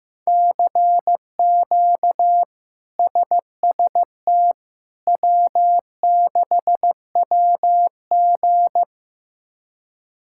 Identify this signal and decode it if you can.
Morse code, heard as a keyed tone, about 15 wpm, 700 Hz. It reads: CQ SST W6WG
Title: K1USN SST QSO Practice - 15wpm